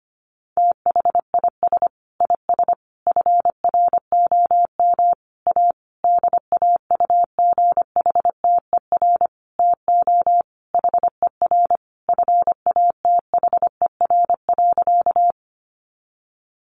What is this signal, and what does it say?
Morse code, heard as a keyed tone, about 25 wpm, 700 Hz. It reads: THIS IS FROM A DAUGHTER TO HER FATHER.
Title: Sentences from Top 1500 Words 25wpm